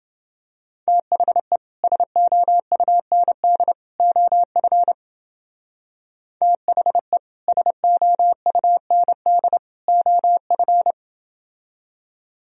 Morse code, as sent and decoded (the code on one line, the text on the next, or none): - .... / ... --- ..- -. -.. / --- ..-. / - .... / ... --- ..- -. -.. / --- ..-.
THE SOUND OF THE SOUND OF